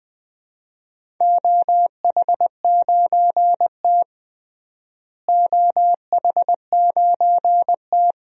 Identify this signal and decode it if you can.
Morse code, heard as a keyed tone, about 20 wpm, 700 Hz. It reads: OH9T OH9T
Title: DXCC-100 Call Signs - 4 Characters - Repeated 2x - 20wpm